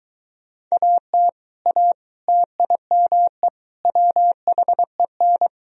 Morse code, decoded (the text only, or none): AT A TIME WHEN